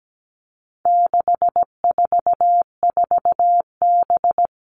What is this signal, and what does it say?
Morse code, heard as a keyed tone, about 17 wpm, 700 Hz. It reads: 644B
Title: All Cut Numbers - 4-Digit 17wpm